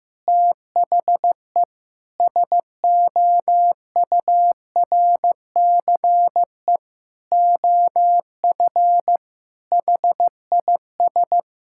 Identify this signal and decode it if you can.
Morse code, heard as a keyed tone, about 15 wpm, 700 Hz. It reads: THE SOURCE OF HIS
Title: Sentences from Top 1000 Words - Review Part 5 - 15wpm